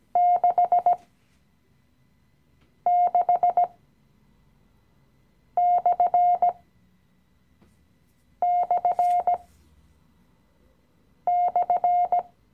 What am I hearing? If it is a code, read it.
Morse code, heard as a keyed tone, about 17 wpm, spaced as for 4 wpm, 710 Hz. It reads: 66///
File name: CW-Online-Kurs / Anfänger / Lektion 016